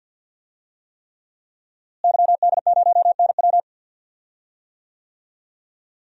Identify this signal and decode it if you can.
Morse code, heard as a keyed tone, about 46 wpm, 700 Hz. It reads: YD0NW
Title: DXCC-100 Call Signs - 5 Characters - Repeated 3x - 50wpm